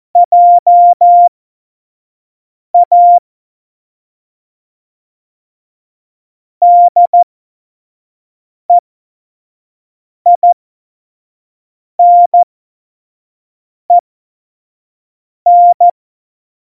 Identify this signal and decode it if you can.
Morse code, heard as a keyed tone, about 14 wpm, spaced as for 5 wpm, 700 Hz. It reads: JA DEINEN
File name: Morsekurs / Witze / Witz 25. Juni 2019, 25 F70